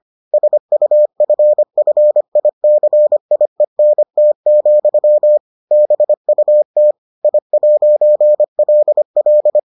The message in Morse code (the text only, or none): SUFFICIENT, BUT I'LL